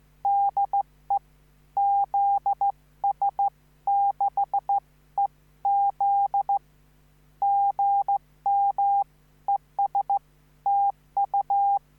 Morse code, heard as a keyed tone, about 14 wpm, 800 Hz. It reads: DE ZS6EZ GM ES TU
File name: QSO03